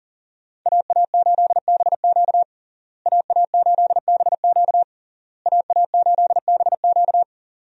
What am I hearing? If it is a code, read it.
Morse code, heard as a keyed tone, about 40 wpm, 700 Hz. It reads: AA8BQ AA8BQ AA8BQ
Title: USA Call Signs - 5 Characters - Repeated 3x - 40wpm